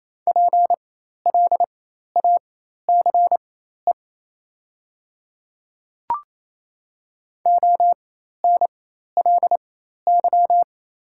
Morse code, heard as a keyed tone, about 28 wpm, spaced as for 13 wpm, 700 Hz. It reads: PLACE ONLY
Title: 4x Character Spacing - Top 200 Words - 28wpm